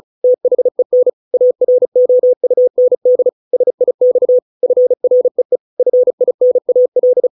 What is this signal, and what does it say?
Morse code, heard as a keyed tone, about 35 wpm, 500 Hz. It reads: THEN AROUND SIX FREE FINAL